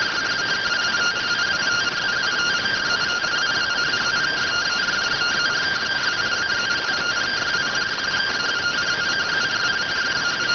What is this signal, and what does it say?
Two-tone teleprinter signal, about 170 Hz shift.